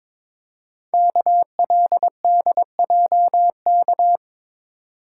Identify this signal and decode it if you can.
Morse code, heard as a keyed tone, about 22 wpm, 700 Hz. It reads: KLDJK
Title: Code Groups - 5 Letters - 22wpm